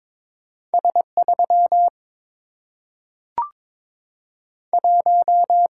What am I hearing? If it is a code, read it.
Morse code, heard as a keyed tone, about 22 wpm, 700 Hz. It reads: S 3 1
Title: Single-Letter Single-Number 22wpm